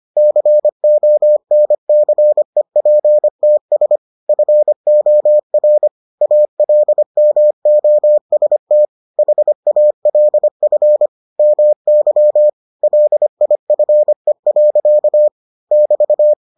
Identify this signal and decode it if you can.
Morse code, heard as a keyed tone, about 25 wpm, 600 Hz. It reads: CONCEPTS FOR ALMOST HALF MY LIFE. <BT>